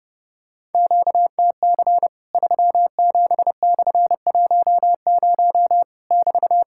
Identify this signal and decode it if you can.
Morse code, heard as a keyed tone, about 30 wpm, 700 Hz. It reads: QTC 37/10 <BT>
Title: Worked All Europe DX Contest - QTC Exchange - 30wpm